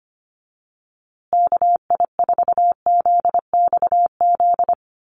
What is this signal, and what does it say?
Morse code, heard as a keyed tone, about 25 wpm, 700 Hz. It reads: KI4ZXZ
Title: USA Call Signs Repeated 2x 25wpm